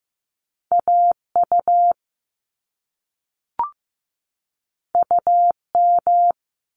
Morse code, heard as a keyed tone, about 15 wpm, 700 Hz. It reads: AU UM
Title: Top Two Letter Combinations in Words 15wpm